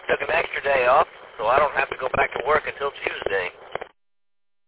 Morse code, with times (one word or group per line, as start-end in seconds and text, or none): none